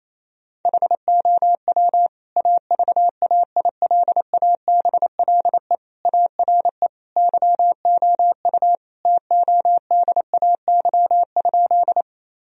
Morse code, heard as a keyed tone, about 28 wpm, 700 Hz. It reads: HOW AVAILABLE ARE YOU TODAY?
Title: Sentences from Top 1000 Words - Review Part 3 - 28wpm